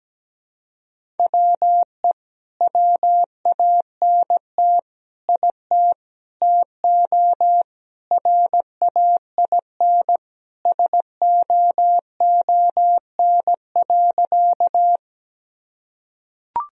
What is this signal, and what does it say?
Morse code, heard as a keyed tone, about 17 wpm, 700 Hz. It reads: WE WANT IT TO RAIN SOON.
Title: Sentences from Top 1000 Words - Review Part 1 - 17wpm